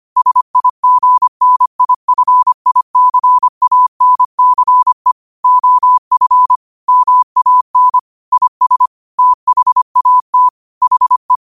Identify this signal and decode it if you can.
Morse code, heard as a keyed tone, about 25 wpm, 1000 Hz. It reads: SIGNIFICANCE OF MAN IS THAT HE
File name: 25wpm